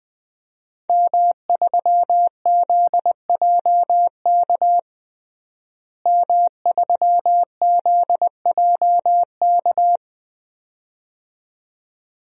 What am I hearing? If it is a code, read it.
Morse code, heard as a keyed tone, about 20 wpm, 700 Hz. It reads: M3ZJK M3ZJK
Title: DXCC-100 Call Signs - 5 Characters - Repeated 2x - 20wpm